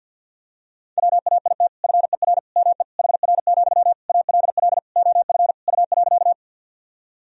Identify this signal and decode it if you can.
Morse code, heard as a keyed tone, about 46 wpm, 700 Hz. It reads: WAIT FER ME HR, ALL OF U.
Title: Sentences from Top 500 Words and Abbreviations 50wpm